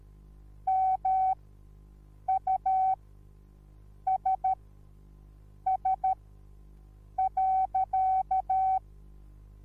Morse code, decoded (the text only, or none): MUSS.